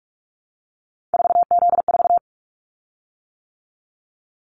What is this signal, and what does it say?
Morse code, heard as a keyed tone, about 46 wpm, 700 Hz. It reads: 4Z4
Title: Callsign Prefixes for Top 50 DXCC - 45wpm